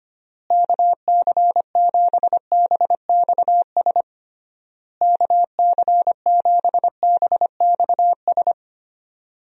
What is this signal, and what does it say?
Morse code, heard as a keyed tone, about 25 wpm, 700 Hz. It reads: KC7BXS KC7BXS
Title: USA Call Signs - 6 Characters - Repeated 3x - 25wpm